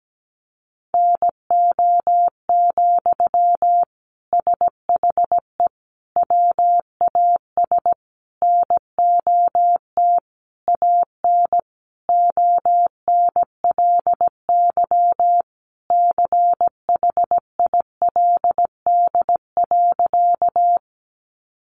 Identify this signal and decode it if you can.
Morse code, heard as a keyed tone, about 17 wpm, 700 Hz. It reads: NO, SHE WAS NOT AN ONLY CHILD.
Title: Sentences from Top 800 Words 17wpm